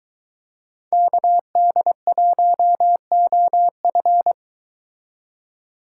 Morse code, heard as a keyed tone, about 22 wpm, 700 Hz. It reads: KD1OF